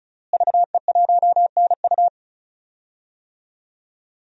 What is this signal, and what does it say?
Morse code, heard as a keyed tone, about 36 wpm, 700 Hz. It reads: VE1NU
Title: DXCC-100 Call Signs - 5 Characters - Repeated 2x - 35wpm